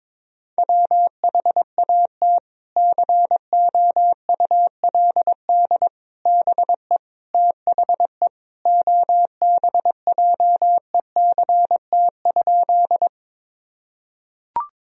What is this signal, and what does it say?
Morse code, heard as a keyed tone, about 22 wpm, 700 Hz. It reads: WHAT COULD BE THE OBJECT?
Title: Sentences from Top 1000 Words - Review Part 3 - 22wpm